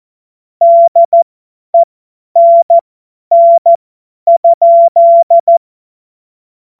Morse code, heard as a keyed tone, about 14 wpm, spaced as for 10 wpm, 680 Hz. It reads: DENN?
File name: Morsekurs / Witze / Witz 22. August 2023 50 F70